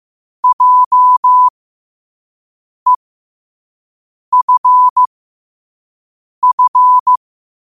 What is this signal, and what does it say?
Morse code, heard as a keyed tone, about 15 wpm, spaced as for 5 wpm, 1000 Hz. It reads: JEFF
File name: qotdfiles / 5wpm